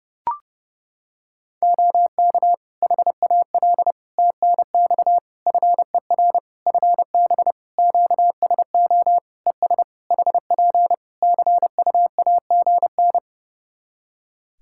Morse code, heard as a keyed tone, about 30 wpm, 700 Hz. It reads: OK HAL TNX FER FB QSO ES HP CUAGN
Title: Practice QSO - Exchange 4 - Ending - 30wpm